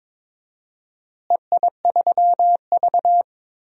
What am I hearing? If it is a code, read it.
Morse code, heard as a keyed tone, about 22 wpm, 700 Hz. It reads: EI3V
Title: DXCC-100 Call Signs - 4 Characters - Repeated 3x - 22wpm